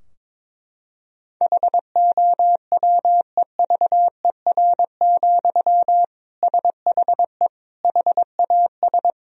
Morse code, heard as a keyed tone, about 22 wpm, 700 Hz. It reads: HOWEVER, SHE HAS
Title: Sentences from Top 1000 Words - Review Part 2 - 22wpm